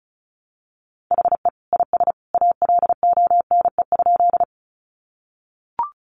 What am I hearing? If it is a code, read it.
Morse code, heard as a keyed tone, about 34 wpm, 700 Hz. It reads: HE IS ALONE?
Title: Sentences from Top 1000 Words 35wpm